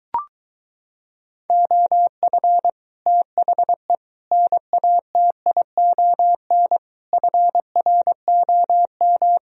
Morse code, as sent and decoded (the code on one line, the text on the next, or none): --- ..-. / - .... / -. .- - .. --- -. / ..-. .-. --- --
OF THE NATION FROM